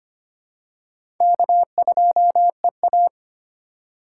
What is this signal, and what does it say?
Morse code, heard as a keyed tone, about 25 wpm, 700 Hz. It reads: K2EA
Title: USA Call Signs - 4 Characters - 3x Character Spacing - 25wpm